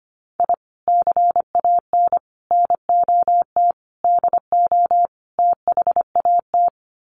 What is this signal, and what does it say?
Morse code, heard as a keyed tone, about 25 wpm, 700 Hz. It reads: I CAN NOT DO THAT
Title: Sets of 5 Words Encore2 25wpm